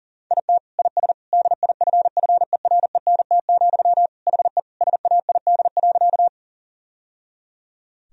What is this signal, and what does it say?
Morse code, heard as a keyed tone, about 40 wpm, 700 Hz. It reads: IT IS DIFFERENT, HE SAID.
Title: Sentences from Top 1000 Words - Review Part 3 - 40wpm